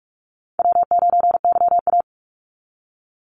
Morse code, heard as a keyed tone, about 44 wpm, 700 Hz. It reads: W9YA
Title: Easy Call Signs Repeated 3x - 45wpm